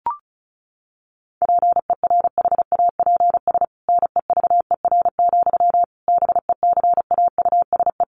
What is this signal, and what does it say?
Morse code, heard as a keyed tone, about 34 wpm, 700 Hz. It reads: PERHAPS NEVER, BECAUSE
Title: Sentences from Top 1200 Words 35wpm